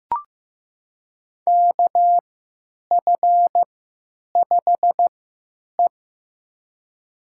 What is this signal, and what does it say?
Morse code, heard as a keyed tone, about 15 wpm, spaced as for 9 wpm, 700 Hz. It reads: KF5E
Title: USA Call Signs - 4 Characters - 3x Character Spacing - 15wpm